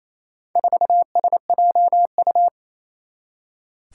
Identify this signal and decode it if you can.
Morse code, heard as a keyed tone, about 28 wpm, 700 Hz. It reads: 4SJU
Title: Four Challenging Letter-Number Combinations 28wpm